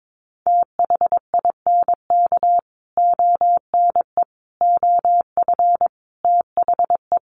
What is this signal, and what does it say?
Morse code, heard as a keyed tone, about 22 wpm, 700 Hz. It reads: THINK ONE OF THE
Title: Sets of 5 Words Encore2 22wpm